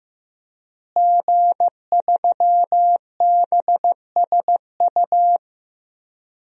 Morse code, heard as a keyed tone, about 15 wpm, 700 Hz. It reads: G3BSU
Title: DXCC-100 Call Signs - 5 Characters - Repeated 3x - 15wpm